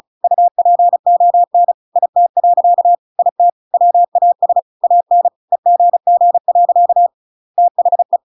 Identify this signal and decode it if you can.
Morse code, heard as a keyed tone, about 35 wpm, 700 Hz. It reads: UPON IT. IT WAS AN EGG. THE